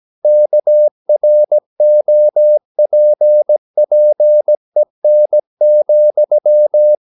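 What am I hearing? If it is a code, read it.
Morse code, heard as a keyed tone, about 17 wpm, 600 Hz. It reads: KROPPEN,
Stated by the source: Slow (QRS) Morse practice at 17 wpm, from CW med Gnister.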